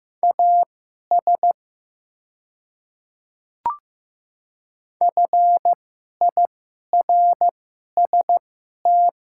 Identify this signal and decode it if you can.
Morse code, heard as a keyed tone, about 15 wpm, spaced as for 11 wpm, 700 Hz. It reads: AS FIRST
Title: 2x Character Spacing - Top 100 Words - 15wpm